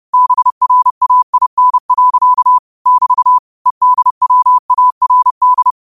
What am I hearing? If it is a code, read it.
Morse code, heard as a keyed tone, about 30 wpm, 1000 Hz. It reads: BRAIN. <BT> EDWARD
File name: qotdfiles / 30wpm